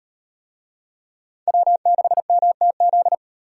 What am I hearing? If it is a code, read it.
Morse code, heard as a keyed tone, about 36 wpm, 700 Hz. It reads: W6MTZ